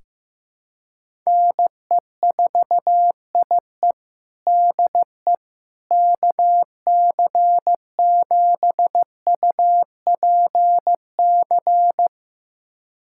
Morse code, heard as a keyed tone, about 15 wpm, 700 Hz. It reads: NE4IE DE KC7UPC
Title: USA Callsign DE Callsign 15wpm